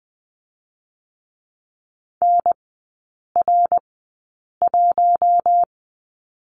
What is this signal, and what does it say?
Morse code, heard as a keyed tone, about 20 wpm, spaced as for 8 wpm, 700 Hz. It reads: NR1
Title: 067 Lesson 12e - Letter d - Three Characters 20wpm